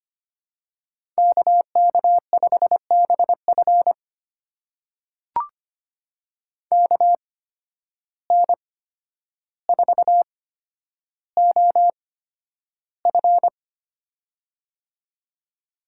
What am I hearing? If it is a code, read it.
Morse code, heard as a keyed tone, about 25 wpm, spaced as for 11 wpm, 700 Hz. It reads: KK5BF KN4OF
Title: USA Call Signs - 5 Characters - 8x Character Spacing - 25wpm